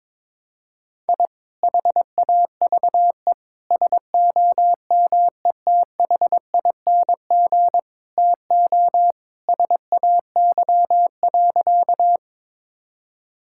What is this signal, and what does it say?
Morse code, heard as a keyed tone, about 22 wpm, 700 Hz. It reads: I HAVE SOMETHING TO SAY.
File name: practice / 22wpm / Sentences with 2-Word Phrases - 22wpm